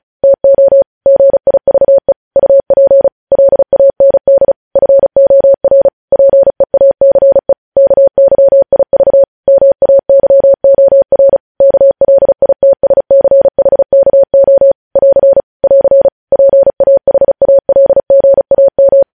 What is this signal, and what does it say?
Morse code, heard as a keyed tone, about 36 wpm, 550 Hz. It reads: TO GIVE UP LAND FOR PEACE KYIV MAYOR KLITSCHKO <AR> <AR> PAHALGAM
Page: News headlines in Morse code at 35 wpm